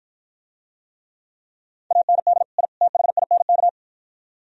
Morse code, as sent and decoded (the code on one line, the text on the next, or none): .- -. -.. / .. / - .... .. -. -.-
AND I THINK